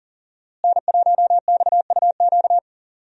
Morse code, heard as a keyed tone, about 40 wpm, 700 Hz. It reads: N1XUQ
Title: USA Call Signs - 5 Characters - Repeated 3x - 40wpm